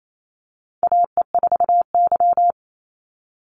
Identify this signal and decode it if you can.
Morse code, heard as a keyed tone, about 28 wpm, 700 Hz. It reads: AE4Y
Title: NA SOTA Chaser Callsigns 28wpm